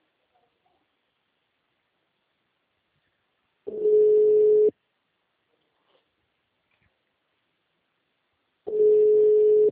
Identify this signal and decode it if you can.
Ringback tone (Europe, 425 Hz).